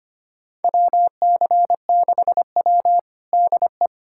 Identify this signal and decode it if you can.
Morse code, heard as a keyed tone, about 25 wpm, 700 Hz. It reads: WC6W DE
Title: USA Callsign DE Callsign - 4 Characters - 25wpm